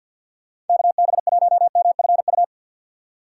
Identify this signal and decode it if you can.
Morse code, heard as a keyed tone, about 52 wpm, 700 Hz. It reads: KB1MFU
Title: USA Call Signs - 3x Character Spacing - 50wpm